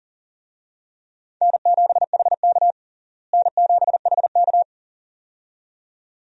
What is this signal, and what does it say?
Morse code, heard as a keyed tone, about 40 wpm, 700 Hz. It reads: N7HK N7HK
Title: USA Call Signs - 4 Characters - Repeated 2x - 40wpm